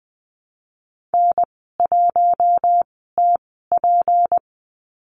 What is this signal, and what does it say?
Morse code, heard as a keyed tone, about 20 wpm, spaced as for 14 wpm, 700 Hz. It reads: N1TP
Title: 213 Lesson 34g - Number 8 - Call Signs 20wpm